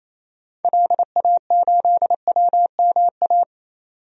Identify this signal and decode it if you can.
Morse code, heard as a keyed tone, about 28 wpm, 700 Hz. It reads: LA8WMA
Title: DXCC-100 Call Signs - 6 Characters - 28wpm